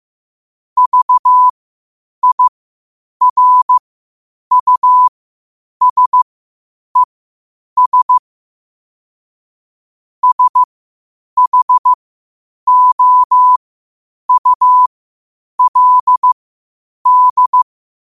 Morse code, heard as a keyed tone, about 15 wpm, spaced as for 8 wpm, 1000 Hz. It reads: VIRUSES SHOULD
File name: qotdfiles / 8wpm